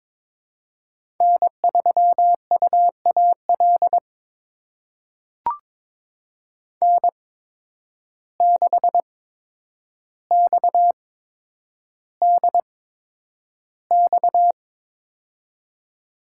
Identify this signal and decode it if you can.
Morse code, heard as a keyed tone, about 22 wpm, spaced as for 9 wpm, 700 Hz. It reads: N3UAL N6XDX
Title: USA Call Signs - 5 Characters - 8x Character Spacing - 22wpm